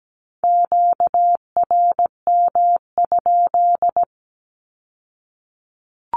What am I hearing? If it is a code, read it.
Morse code, heard as a keyed tone, about 17 wpm, 700 Hz. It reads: QRM?
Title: Q-Codes with Spoken Meaning - 17wpm